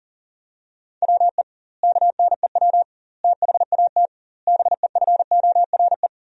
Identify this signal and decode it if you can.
Morse code, heard as a keyed tone, about 40 wpm, 700 Hz. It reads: WE KNEW THAT BEFORE
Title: 2x Word-Spacing - Sentences from Top 400 Words - 40wpm